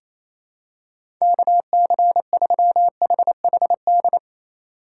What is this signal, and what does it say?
Morse code, heard as a keyed tone, about 27 wpm, 700 Hz. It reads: KC3HHD
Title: USA Call Signs - 6 Characters - 2x Character Spacing - 28wpm